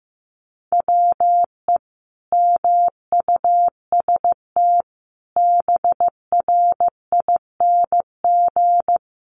Morse code, heard as a keyed tone, about 15 wpm, 700 Hz. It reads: WE MUST BRING
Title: Sentences from Top 1200 Words 15wpm